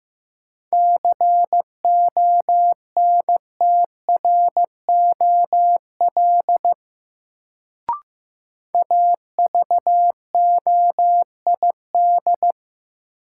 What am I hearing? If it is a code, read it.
Morse code, heard as a keyed tone, about 15 wpm, 700 Hz. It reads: CONTROL AVOID
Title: Sentences from Top 1200 Words - New Word Drill Alt - 15wpm